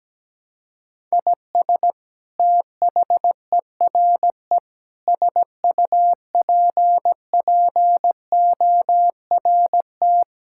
Morse code, heard as a keyed tone, about 17 wpm, 700 Hz. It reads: IS THERE SUPPORT
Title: Sentences from Top 1000 Words - Review Part 6 - 17wpm